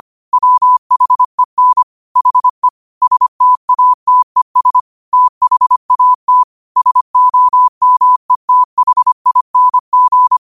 Morse code, heard as a keyed tone, about 25 wpm, 1000 Hz. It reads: WHEN HE STATES THAT SOMETHING
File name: qotdfiles / 25wpm